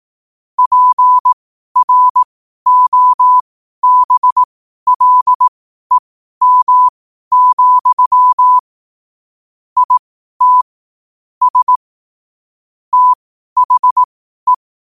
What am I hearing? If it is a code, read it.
Morse code, heard as a keyed tone, about 18 wpm, spaced as for 12 wpm, 1000 Hz. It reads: PROBLEM, IT S THE
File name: qotdfiles / 12wpm